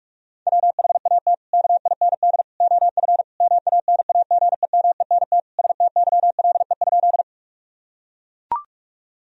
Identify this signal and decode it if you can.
Morse code, heard as a keyed tone, about 43 wpm, 700 Hz. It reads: WHAT KIND OF MANAGEMENT STYLE?
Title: Sentences from Top 1000 Words - Review Part 4 - 45wpm